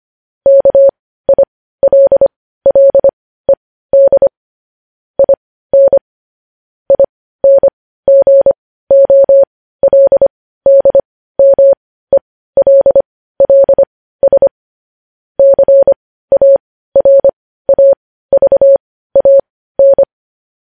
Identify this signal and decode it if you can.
Morse code, heard as a keyed tone, about 25 wpm, spaced as for 15 wpm, 550 Hz. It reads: KILLED IN INGOLDMELLS CARAVAN
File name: podcast / News headlines-1x-25x15